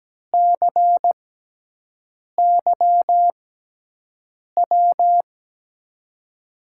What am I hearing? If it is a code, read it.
Morse code, heard as a keyed tone, about 17 wpm, spaced as for 6 wpm, 700 Hz. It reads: CYW
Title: USA Call Signs - 5 Characters - 6x Character Spacing - 17wpm